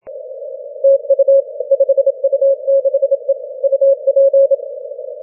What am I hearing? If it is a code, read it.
Morse code, heard as a keyed tone, about 27 wpm, 550 Hz. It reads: TU 5U6E UP
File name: wp / 5u6e 10m cw 06 oct 2012